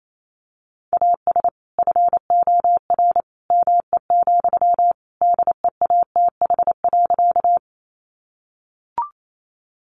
Morse code, heard as a keyed tone, about 28 wpm, 700 Hz. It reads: AS FOR ME, DEATH.
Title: Sentences from Top 1100 Words 28wpm